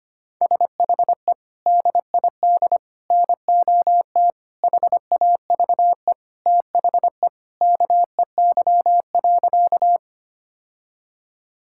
Morse code, heard as a keyed tone, about 25 wpm, 700 Hz. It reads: SHE DID NOT HAVE THE KEY.
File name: practice / 25wpm / Sentences with 4-Word Phrases - 25wpm